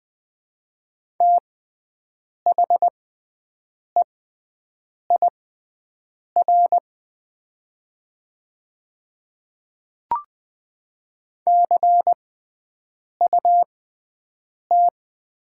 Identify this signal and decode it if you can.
Morse code, heard as a keyed tone, about 20 wpm, spaced as for 7 wpm, 700 Hz. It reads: THEIR CUT
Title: 6x Character Spacing - Top 300 Words - 20wpm